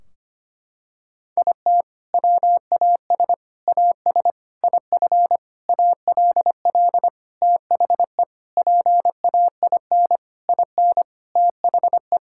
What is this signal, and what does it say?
Morse code, heard as a keyed tone, about 25 wpm, 700 Hz. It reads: IT WAS AS IF ALL THE PAIN IN THE
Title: Sentences from Top 1600 Words 25wpm